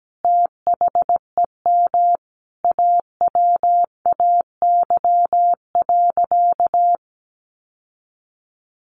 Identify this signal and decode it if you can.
Morse code, heard as a keyed tone, about 17 wpm, 700 Hz. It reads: THEM AWAY.
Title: Sentences from Top 1500 Words 17wpm